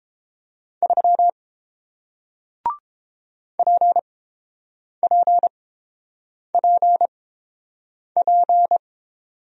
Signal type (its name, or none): Morse code